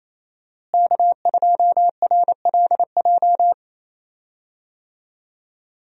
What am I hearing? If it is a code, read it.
Morse code, heard as a keyed tone, about 27 wpm, 700 Hz. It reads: K2RLJ
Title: USA Call Signs - 5 Characters - Repeated 2x - 28wpm